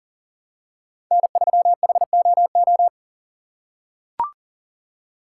Morse code, heard as a keyed tone, about 40 wpm, 700 Hz. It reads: N3HOO
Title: USA Call Signs - 5 Characters - 2x Character Spacing - 40wpm